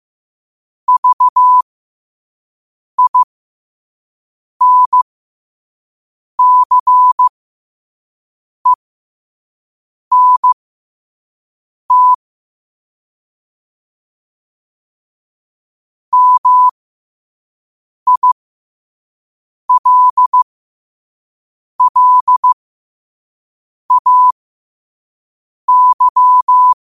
Morse code, heard as a keyed tone, about 15 wpm, spaced as for 5 wpm, 1000 Hz. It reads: VINCENT MILLAY
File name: qotdfiles / 5wpm